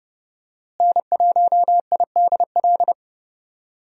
Morse code, heard as a keyed tone, about 31 wpm, 700 Hz. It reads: N1IDL
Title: USA Call Signs - 5 Characters - Repeated 2x - 30wpm